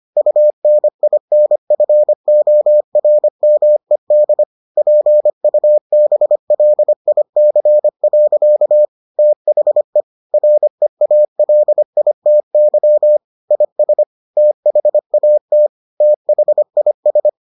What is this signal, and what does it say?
Morse code, heard as a keyed tone, about 25 wpm, 600 Hz. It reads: UNINFORMED PUBLIC. THE REALITY IS THAT THIS